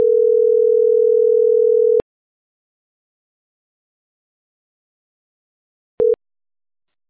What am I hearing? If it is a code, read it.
Ringback tone (North America), 440 + 480 Hz.